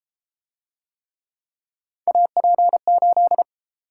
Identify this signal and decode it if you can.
Morse code, heard as a keyed tone, about 32 wpm, 700 Hz. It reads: AP8